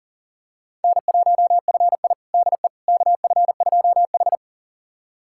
Morse code, heard as a keyed tone, about 37 wpm, 700 Hz. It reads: N1FI DE KF2H
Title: USA Callsign DE Callsign - 4 Characters - 40wpm